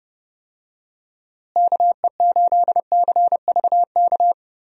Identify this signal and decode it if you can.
Morse code, heard as a keyed tone, about 30 wpm, 700 Hz. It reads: KE8CVK
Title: USA Call Signs - 6 Characters - Repeated 2x - 30wpm